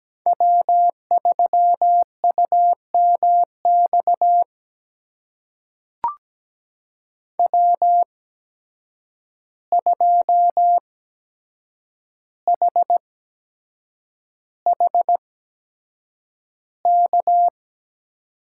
Morse code, heard as a keyed tone, about 17 wpm, spaced as for 7 wpm, 700 Hz. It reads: W3UMX W2HHK
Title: USA Call Signs - 5 Characters - 8x Character Spacing - 17wpm